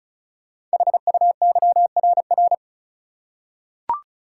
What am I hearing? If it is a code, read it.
Morse code, heard as a keyed tone, about 36 wpm, 700 Hz. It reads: HUYRR
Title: Code Groups - 5 Letters - 35wpm